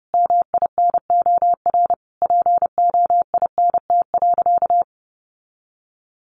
Morse code, heard as a keyed tone, about 30 wpm, 700 Hz. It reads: MINOR POINT.
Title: Sentences from Top 1500 Words 30wpm